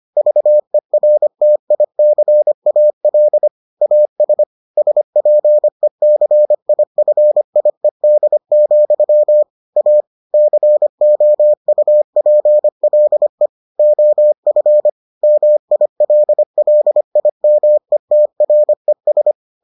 Morse code, heard as a keyed tone, about 25 wpm, 600 Hz. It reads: VERTICAL AS SPECIFIED, A COUPLE OF MILLIMETRES